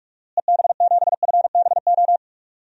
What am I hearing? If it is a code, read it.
Morse code, heard as a keyed tone, about 46 wpm, 700 Hz. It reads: EB7FBO
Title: DXCC-100 Call Signs - 6 Characters - Repeated 2x - 45wpm